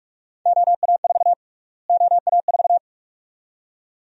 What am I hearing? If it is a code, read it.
Morse code, heard as a keyed tone, about 46 wpm, 700 Hz. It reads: OA4 OA4
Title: USA Call Sign Prefixes - Repeated 2x - 45wpm